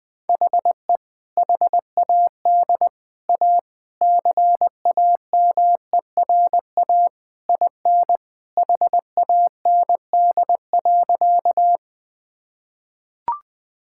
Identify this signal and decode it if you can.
Morse code, heard as a keyed tone, about 20 wpm, 700 Hz. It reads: HE HAD A CAMERA IN HAND.
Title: Sentences from Top 1000 Words - Review Part 5 - 20wpm